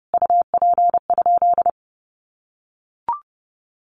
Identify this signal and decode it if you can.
Morse code, heard as a keyed tone, about 30 wpm, 700 Hz. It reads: UP?
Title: Sentences from Top 1300 Words 30wpm